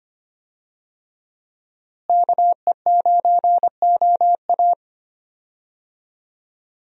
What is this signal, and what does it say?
Morse code, heard as a keyed tone, about 25 wpm, 700 Hz. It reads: KE9OA
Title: USA Call Signs - 5 Characters - Repeated 3x - 25wpm